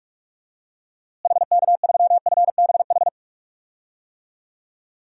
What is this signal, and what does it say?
Morse code, heard as a keyed tone, about 43 wpm, 700 Hz. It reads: HK3FBH
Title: DXCC-100 Call Signs - 6 Characters - Repeated 2x - 45wpm